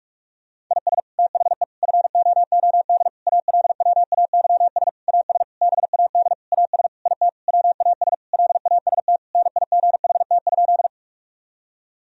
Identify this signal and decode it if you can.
Morse code, heard as a keyed tone, about 46 wpm, 700 Hz. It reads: IS THE FOOD ALWAYS AS BAD AS IT WAS LAST NIGHT?
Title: Sentences from Top 1000 Words - Review Part 1 - 45wpm